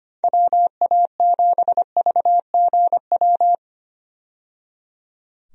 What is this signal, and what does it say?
Morse code, heard as a keyed tone, about 25 wpm, 700 Hz. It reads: WA7VGW
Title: USA Call Signs - 6 Characters - 25wpm